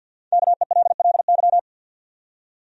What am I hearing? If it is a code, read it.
Morse code, heard as a keyed tone, about 51 wpm, 700 Hz. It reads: KELLY
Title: USA Common Names - Max 5 Letters - 50wpm